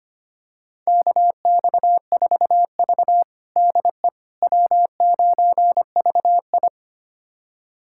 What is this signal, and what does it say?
Morse code, heard as a keyed tone, about 25 wpm, 700 Hz. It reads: KX4V DE W9VI
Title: USA Callsign DE Callsign - 4 Characters - 25wpm